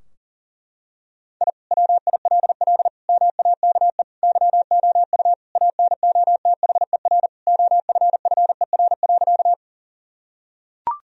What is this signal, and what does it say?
Morse code, heard as a keyed tone, about 37 wpm, 700 Hz. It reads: I WILL MAKE YOU ANOTHER OFFER.
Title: Sentences from Top 1000 Words - Review Part 4 - 40wpm